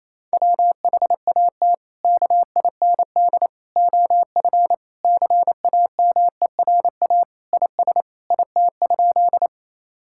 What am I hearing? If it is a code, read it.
Morse code, heard as a keyed tone, about 28 wpm, 700 Hz. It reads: WHAT KIND OF CAMERA IS IT?
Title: Sentences from Top 1000 Words - Review Part 5 - 28wpm